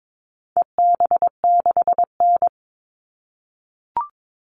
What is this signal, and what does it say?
Morse code, heard as a keyed tone, about 22 wpm, 700 Hz. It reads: EB6N
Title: All Cut Numbers - 4-Digit 22wpm